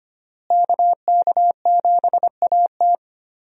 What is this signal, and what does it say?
Morse code, heard as a keyed tone, about 25 wpm, 700 Hz. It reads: KK7AT
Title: USA Call Signs - 5 Characters - Repeated 3x - 25wpm